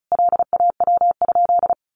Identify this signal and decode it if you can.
Morse code, heard as a keyed tone, about 34 wpm, 700 Hz. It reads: LAW?
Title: Sentences from Top 600 Words 35wpm